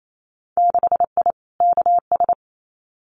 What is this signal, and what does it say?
Morse code, heard as a keyed tone, about 28 wpm, 700 Hz. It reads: 6I KS
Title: Winter Field Day Exchanges 28wpm